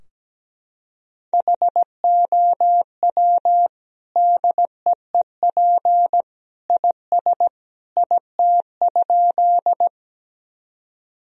Morse code, heard as a keyed tone, about 17 wpm, 700 Hz. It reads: HOW DEEP IS IT?
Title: Sentences from Top 1000 Words - Review Part 3 - 17wpm